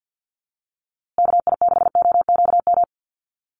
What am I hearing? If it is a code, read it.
Morse code, heard as a keyed tone, about 52 wpm, 700 Hz. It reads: KI6OQM
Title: USA Call Signs Repeated 3x 50wpm